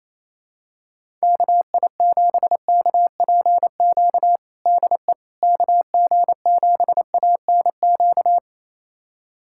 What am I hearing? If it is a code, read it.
Morse code, heard as a keyed tone, about 28 wpm, 700 Hz. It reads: KI7KPQ DE KG7ANQ
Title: USA Callsign DE Callsign - 6 Characters - 28wpm